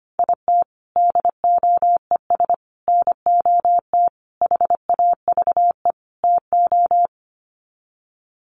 Morse code, heard as a keyed tone, about 25 wpm, 700 Hz. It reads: IT DOES NOT HAVE TO
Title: Sets of 5 Words 25wpm